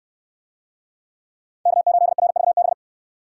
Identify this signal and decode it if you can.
Morse code, heard as a keyed tone, about 61 wpm, 700 Hz. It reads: K8RVB